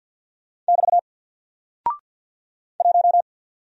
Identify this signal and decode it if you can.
Morse code, heard as a keyed tone, about 49 wpm, 700 Hz. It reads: <BT> 1
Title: Single Character (Letter:Number:Punctuation:Prosign) Alt 50wpm